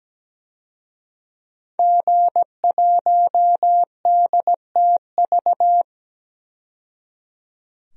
Morse code, heard as a keyed tone, about 17 wpm, 700 Hz. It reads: G1DTV
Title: DXCC-100 Call Signs - 5 Characters - 17wpm